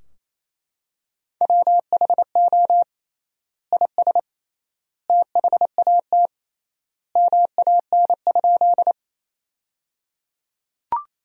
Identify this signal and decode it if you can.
Morse code, heard as a keyed tone, about 28 wpm, 700 Hz. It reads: WHO IS THAT MAN?
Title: 3x Word-Spacing - Sentences from Top 200 Words - 28wpm